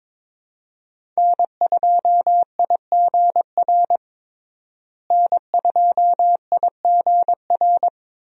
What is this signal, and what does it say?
Morse code, heard as a keyed tone, about 22 wpm, 700 Hz. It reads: N2IGR N2IGR
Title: USA Call Signs - 5 Characters - Repeated 2x - 22wpm